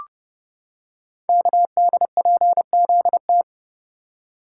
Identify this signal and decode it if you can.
Morse code, heard as a keyed tone, about 28 wpm, 700 Hz. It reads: KDPZT
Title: Code Groups - 5 Letters - 30wpm